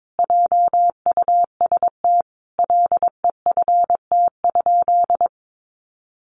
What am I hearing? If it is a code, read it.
Morse code, heard as a keyed tone, about 22 wpm, 700 Hz. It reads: JUST LEFT?
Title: Sentences from Top 1200 Words 22wpm